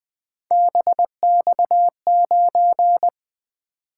Morse code, heard as a keyed tone, about 23 wpm, 700 Hz. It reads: BX9
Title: Speed-Racing - Three Letter-Number Combinations - 20wpm